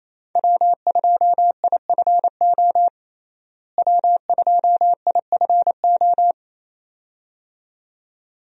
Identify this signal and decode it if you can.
Morse code, heard as a keyed tone, about 28 wpm, 700 Hz. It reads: W2IFO W2IFO
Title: USA Call Signs - 5 Characters - Repeated 2x - 28wpm